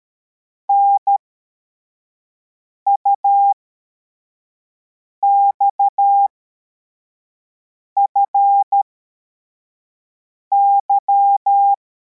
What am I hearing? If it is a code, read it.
Morse code, heard as a keyed tone, about 13 wpm, spaced as for 4 wpm, 800 Hz. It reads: NUXFY
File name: formation / cw / Text64